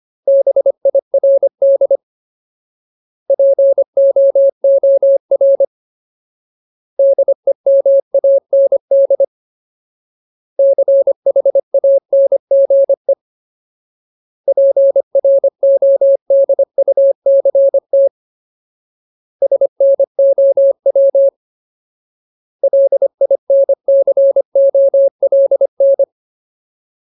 Morse code, heard as a keyed tone, about 25 wpm, 550 Hz. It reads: BIRD POOR DEMAND CHANGE PRODUCT SNOW LINCOLN
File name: Ultimate Word List / 25WPM Ultimate